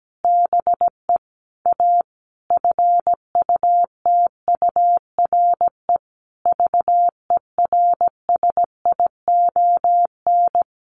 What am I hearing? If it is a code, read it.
Morse code, heard as a keyed tone, about 17 wpm, 700 Hz. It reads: BE A FUTURE VERSION
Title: Sentences from Top 1400 Words 17wpm